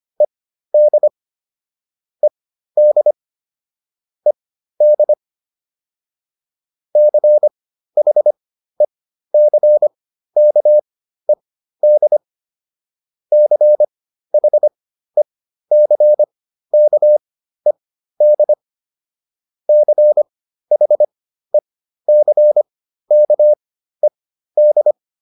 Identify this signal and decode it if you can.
Morse code, heard as a keyed tone, about 25 wpm, spaced as for 13 wpm, 600 Hz. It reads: ED ED ED CHECKED CHECKED CHECKED